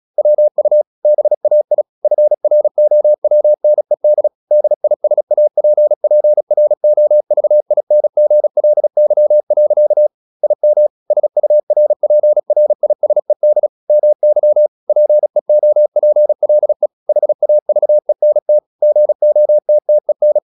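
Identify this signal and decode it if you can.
Morse code, heard as a keyed tone, about 36 wpm, 600 Hz. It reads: WU BAI FROWNED DISAPPROVINGLY. IM SURPRISED MY PEOPLE HAVENT GOTTEN
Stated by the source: Morse practice at 36 wpm, text of Ken Follett's novel Never, from CW med Gnister.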